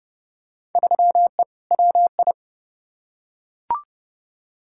Morse code, heard as a keyed tone, about 30 wpm, 700 Hz. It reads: 3E WI
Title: Field Day Exchanges Encore - 30wpm